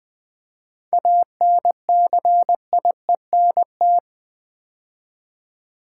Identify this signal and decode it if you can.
Morse code, heard as a keyed tone, about 20 wpm, 700 Hz. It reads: ANCIENT
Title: 7 Letter Words - First 3 Letters Given - 20wpm